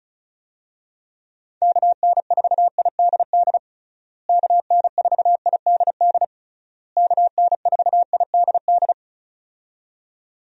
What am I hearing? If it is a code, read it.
Morse code, heard as a keyed tone, about 34 wpm, 700 Hz. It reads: KN4IDD KN4IDD KN4IDD
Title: USA Call Signs - 6 Characters - Repeated 3x - 35wpm